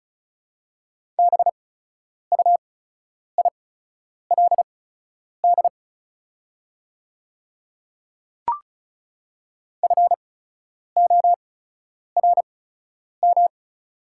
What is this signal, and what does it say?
Morse code, heard as a keyed tone, about 34 wpm, spaced as for 10 wpm, 700 Hz. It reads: BUILD FORM
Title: 8x Character Spacing - Top 200 Words - 35wpm